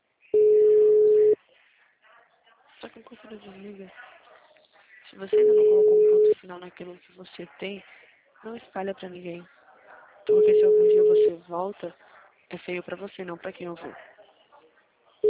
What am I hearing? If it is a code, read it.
Ringback tone (Europe, 425 Hz), with speech.
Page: Atibaia - SP 1